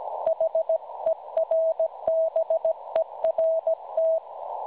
Morse code, heard as a keyed tone, about 16 wpm, 650 Hz. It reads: HERBERT
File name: cwronde / websdr recording .6kHz